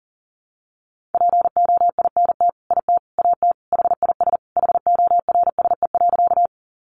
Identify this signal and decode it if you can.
Morse code, heard as a keyed tone, about 40 wpm, 700 Hz. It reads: POINT IT AT HIS HORSE.
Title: Sentences from Top 300 Words 40wpm